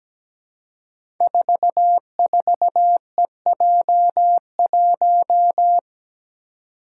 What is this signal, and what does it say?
Morse code, heard as a keyed tone, about 17 wpm, 700 Hz. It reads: 44EJ1
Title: Code Groups - 5 Letter:Numbers - 17wpm